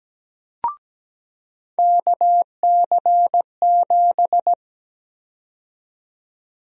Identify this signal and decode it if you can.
Morse code, heard as a keyed tone, about 17 wpm, 700 Hz. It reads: KC7
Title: USA Call Sign Prefixes - 17wpm